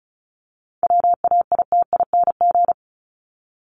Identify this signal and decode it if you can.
Morse code, heard as a keyed tone, about 34 wpm, 700 Hz. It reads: WAITING
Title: ING Suffix - Top 100 Words with Base Words 35wpm